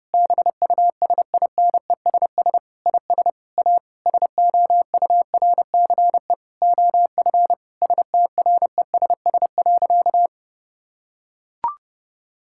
Morse code, heard as a keyed tone, about 30 wpm, 700 Hz. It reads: BUSINESS IS A SOURCE OF STRESS.
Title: Sentences from Top 1000 Words - Review Part 6 - 30wpm